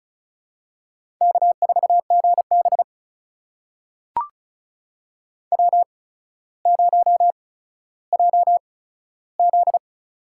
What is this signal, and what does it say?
Morse code, heard as a keyed tone, about 36 wpm, spaced as for 15 wpm, 700 Hz. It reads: K4GB W0JZ
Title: USA Call Signs - 4 Characters - 8x Character Spacing - 35wpm